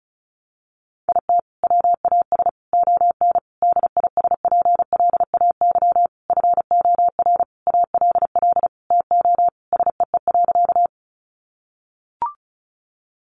Morse code, heard as a keyed tone, about 34 wpm, 700 Hz. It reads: IT WAS ON DISPLAY FOR ALL TO SEE.
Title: Sentences from Top 1200 Words 35wpm